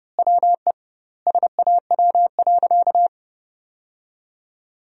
Morse code, heard as a keyed tone, about 30 wpm, 700 Hz. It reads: WE SAW.
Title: 2x Word-Spacing - Sentences from Top 300 Words - 30wpm